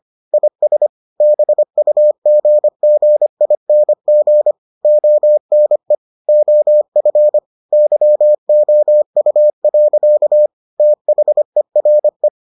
Morse code, heard as a keyed tone, about 25 wpm, 600 Hz. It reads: IS BUGGING ONE OF YOU. THERE